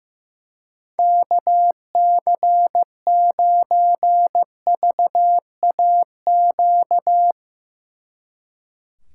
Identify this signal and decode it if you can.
Morse code, heard as a keyed tone, about 15 wpm, 700 Hz. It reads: KC9VAQ
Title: Speed Racing - USA 2x3 Call Signs 15wpm